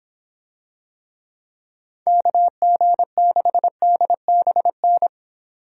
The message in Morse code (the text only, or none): KG6DBN